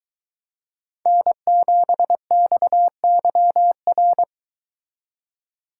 Morse code, heard as a keyed tone, about 23 wpm, 700 Hz. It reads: N7XYR